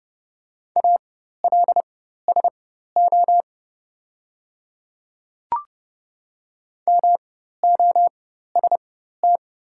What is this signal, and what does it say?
Morse code, heard as a keyed tone, about 31 wpm, spaced as for 14 wpm, 700 Hz. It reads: ALSO MOST
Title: 4x Character Spacing - Top 200 Words - 30wpm